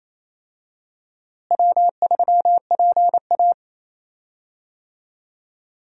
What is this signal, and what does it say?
Morse code, heard as a keyed tone, about 27 wpm, 700 Hz. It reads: W3PA
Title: USA Call Signs - 4 Characters - Repeated 3x - 28wpm